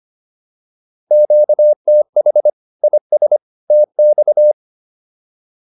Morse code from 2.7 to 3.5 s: IS

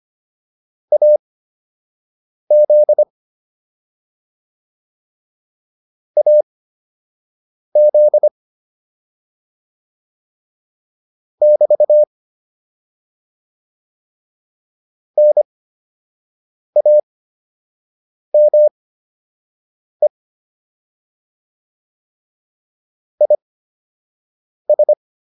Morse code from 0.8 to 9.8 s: AZ AZ

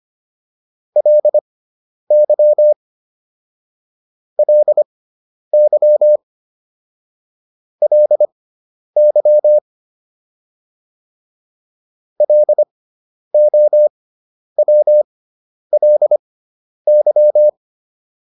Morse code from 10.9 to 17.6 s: LOWLY

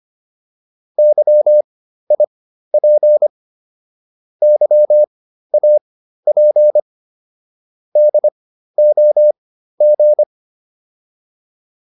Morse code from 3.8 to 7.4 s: YAP

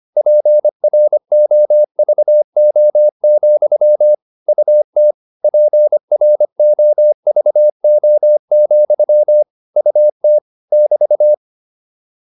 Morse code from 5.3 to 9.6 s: PROVO,

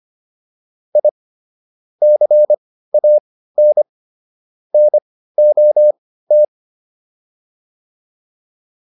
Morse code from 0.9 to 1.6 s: I

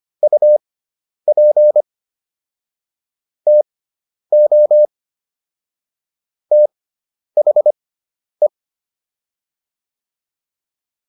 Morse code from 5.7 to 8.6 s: THE

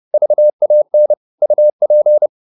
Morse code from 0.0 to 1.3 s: VAN

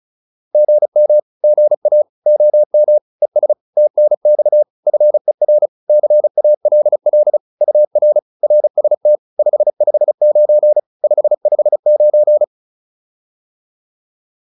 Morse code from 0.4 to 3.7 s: GM GA OM ES

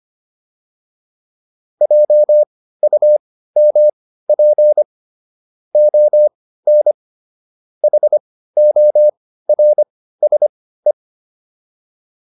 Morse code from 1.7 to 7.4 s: JUMP ON

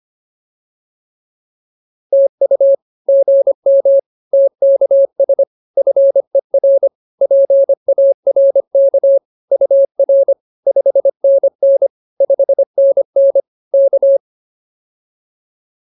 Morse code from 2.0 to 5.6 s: TU GM TKS

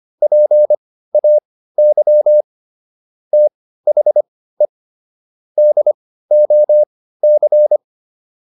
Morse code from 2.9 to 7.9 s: THE DOC

